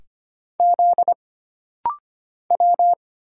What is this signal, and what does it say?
Morse code, heard as a keyed tone, about 25 wpm, 700 Hz. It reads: Z W
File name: Single Character (Letter:Number:Punctuation:Prosign) - Rapid-Fire - 25wpm